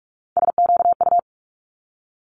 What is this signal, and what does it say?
Morse code, heard as a keyed tone, about 46 wpm, 700 Hz. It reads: SQU
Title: Multi-letter Phonemes 45wpm